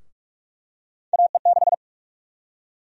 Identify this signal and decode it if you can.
Morse code, heard as a keyed tone, about 47 wpm, 700 Hz. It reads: AE6
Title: USA Call Sign Prefixes - 45wpm